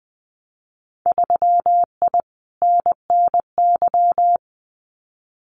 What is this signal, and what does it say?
Morse code, heard as a keyed tone, about 20 wpm, 700 Hz. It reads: 3I NNY
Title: Winter Field Day Exchanges 20wpm